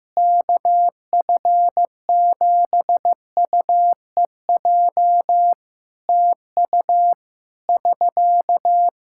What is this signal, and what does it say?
Morse code, heard as a keyed tone, about 15 wpm, 700 Hz. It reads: KF7UEJ TU <SK>
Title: Practice QSO - Exchange 4 - Ending - 15wpm